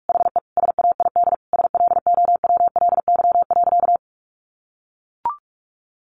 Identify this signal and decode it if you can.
Morse code, heard as a keyed tone, about 44 wpm, 700 Hz. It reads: HE SAID SLOWLY.
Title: Sentences from Top 1100 Words 45wpm